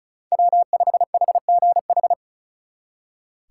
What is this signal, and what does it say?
Morse code, heard as a keyed tone, about 44 wpm, 700 Hz. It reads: W5HGH
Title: Speed Racing - USA 1x3 Call Signs 35wpm